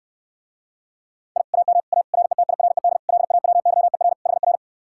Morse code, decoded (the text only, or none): I AM A RETIRED LAWYER HR